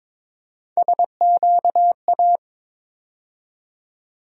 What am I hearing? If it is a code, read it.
Morse code, heard as a keyed tone, about 22 wpm, 700 Hz. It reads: SQA
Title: USA Call Sign Postfixes - 3 Letters 22wpm